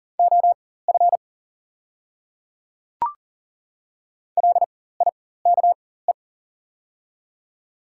Morse code, heard as a keyed tone, about 40 wpm, spaced as for 19 wpm, 700 Hz. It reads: OF LIKE
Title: 4x Character Spacing - Top 100 Words - 40wpm